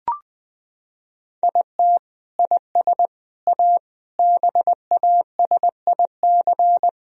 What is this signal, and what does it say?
Morse code, heard as a keyed tone, about 20 wpm, 700 Hz. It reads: IT IS A BASIC
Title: Sentences from Top 1000 Words - Review Part 3 - 20wpm